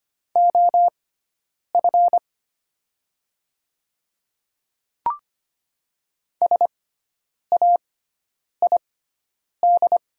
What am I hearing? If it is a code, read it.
Morse code, heard as a keyed tone, about 25 wpm, spaced as for 9 wpm, 700 Hz. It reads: OF SAID
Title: 6x Character Spacing - Top 100 Words - 25wpm